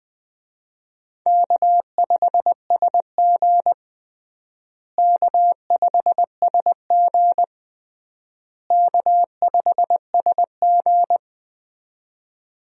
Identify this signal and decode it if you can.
Morse code, heard as a keyed tone, about 20 wpm, 700 Hz. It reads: K5SG K5SG K5SG
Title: USA Call Signs - 4 Characters - Repeated 3x - 20wpm